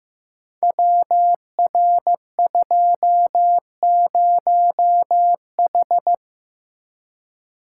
Morse code, heard as a keyed tone, about 15 wpm, 700 Hz. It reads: WR20H
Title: Code Groups - 5 Letter:Numbers - 15wpm